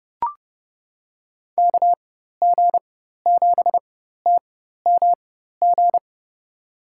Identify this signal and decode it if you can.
Morse code, heard as a keyed tone, about 30 wpm, spaced as for 14 wpm, 700 Hz. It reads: KG7TMG
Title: USA Call Signs - 6 Characters - 4x Character Spacing - 30wpm